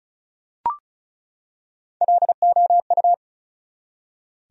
Morse code, heard as a keyed tone, about 34 wpm, 700 Hz. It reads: LOU
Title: USA Common Names - Max 5 Letters - 35wpm